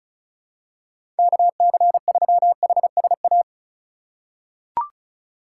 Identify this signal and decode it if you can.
Morse code, heard as a keyed tone, about 34 wpm, 700 Hz. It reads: KC3HSA
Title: USA Call Signs - 6 Characters - 3x Character Spacing - 35wpm